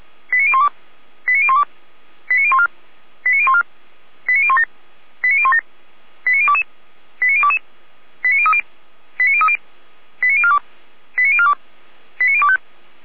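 Five-tone selective call (ZVEI), heard as quick runs of five tones, tones of about 70 ms. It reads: 89012 89012 89014 89014 89017 89017 89022 89022 89030 89030 89042 89042 89025
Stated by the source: ZVEI five-tone call received with an SDR.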